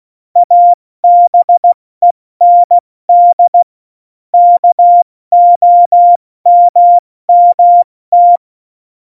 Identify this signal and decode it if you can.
Morse code, heard as a keyed tone, about 16 wpm, 700 Hz. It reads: ABEND KOMMT
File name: Morsekurs / Witze / Witz 15. Oktober 2019, 70 F80